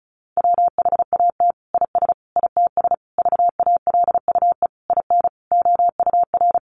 Morse code, heard as a keyed tone, about 34 wpm, 700 Hz. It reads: WHAT IS ITS VALUE IN OUR
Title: Sentences from Top 600 Words 35wpm